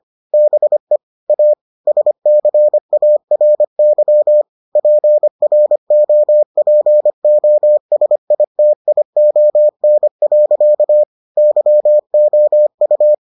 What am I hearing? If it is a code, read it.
Morse code, heard as a keyed tone, about 25 wpm, 600 Hz. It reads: BE A SCARY PROPOSITION. YOU